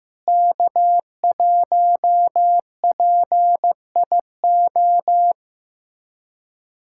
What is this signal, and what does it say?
Morse code, heard as a keyed tone, about 15 wpm, 700 Hz. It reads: K1PIO
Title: USA Callsign DE Callsign 15wpm